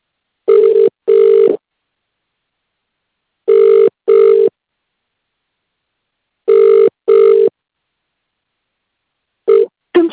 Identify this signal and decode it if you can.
Ringback tone (UK), 400 + 450 Hz.